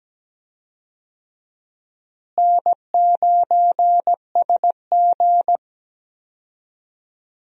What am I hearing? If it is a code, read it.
Morse code, heard as a keyed tone, about 17 wpm, 700 Hz. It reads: N9SG
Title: USA Call Signs - 4 Characters - Repeated 3x - 17wpm